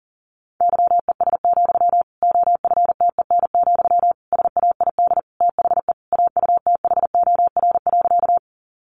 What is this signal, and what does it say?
Morse code, heard as a keyed tone, about 40 wpm, 700 Hz. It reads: YES, OFTEN, SAID THE AUTHOR.
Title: Sentences from Top 1200 Words 40wpm